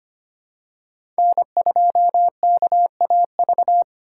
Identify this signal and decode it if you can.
Morse code, heard as a keyed tone, about 25 wpm, 700 Hz. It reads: N2KAV
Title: USA Call Signs - 5 Characters - 2x Character Spacing - 25wpm